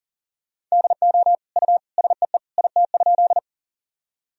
Morse code, heard as a keyed tone, about 40 wpm, 700 Hz. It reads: DO U SEE IT?
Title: Sentences from Top 100 Words and Abbreviations 40wpm